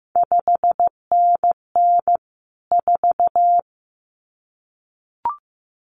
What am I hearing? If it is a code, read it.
Morse code, heard as a keyed tone, about 15 wpm, 700 Hz. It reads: 5NN 4
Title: CQ WW DX Contest Exchange Encore - 15wpm